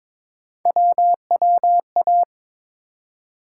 Morse code, heard as a keyed tone, about 21 wpm, 700 Hz. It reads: WWA
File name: ARRL Sections v2 - 22wpm